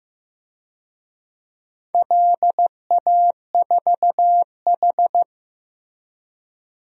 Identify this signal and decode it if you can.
Morse code, heard as a keyed tone, about 15 wpm, 700 Hz. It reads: LA4H
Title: DXCC-100 Call Signs - 4 Characters - Repeated 2x - 15wpm